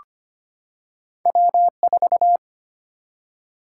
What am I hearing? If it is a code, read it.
Morse code, heard as a keyed tone, about 25 wpm, 700 Hz. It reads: W 4
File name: Two Letter-Number Combinations - 25wpm